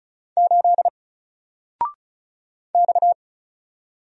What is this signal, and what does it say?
Morse code, heard as a keyed tone, about 36 wpm, 700 Hz. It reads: 8 X
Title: Single Letter-Number Alt 35wpm